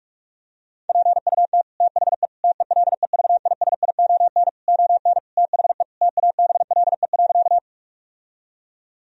Morse code, heard as a keyed tone, about 46 wpm, 700 Hz. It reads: PUT THE TELEVISION ON THE TABLE.
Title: Sentences from Top 1000 Words - Review Part 4 - 45wpm